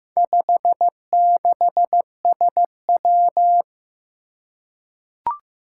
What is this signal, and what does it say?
Morse code, heard as a keyed tone, about 15 wpm, 700 Hz. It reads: 56SW
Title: Four Challenging Letter-Number Combinations 15wpm